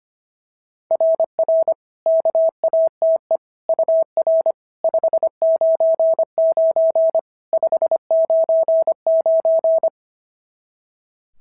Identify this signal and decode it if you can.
Morse code, heard as a keyed tone, about 25 wpm, 650 Hz. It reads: RR KATE UR 599 599
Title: SOTA QSO Practice 25wpm